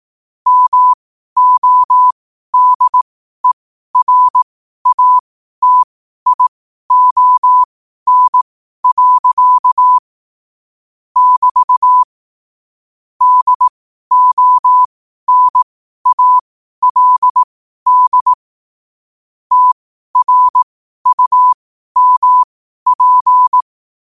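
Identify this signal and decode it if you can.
Morse code, heard as a keyed tone, about 18 wpm, spaced as for 12 wpm, 1000 Hz. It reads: MODERATION. <BT> DONALD TRUMP
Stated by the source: 12 wpm Morse.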